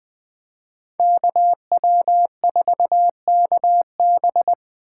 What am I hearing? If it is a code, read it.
Morse code, heard as a keyed tone, about 20 wpm, 700 Hz. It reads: KW4KB
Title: Speed Racing - USA Call Signs 17wpm